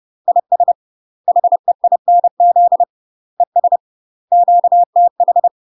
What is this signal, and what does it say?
Morse code, heard as a keyed tone, about 30 wpm, 700 Hz. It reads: IS HEINZ ES QTH
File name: UK MASTER QSOs TRAINING / 30WPM / UK MASTER QSOs 0048